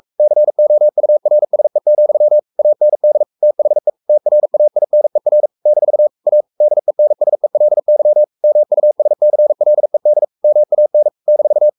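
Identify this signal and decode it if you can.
Morse code, heard as a keyed tone, about 44 wpm, 600 Hz. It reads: COURSE, AND THE TRAINER - A DENSELY MUSCLED MAN -
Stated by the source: Morse code at 43 wpm sending Ken Follett's novel Never in English, from CW med Gnister.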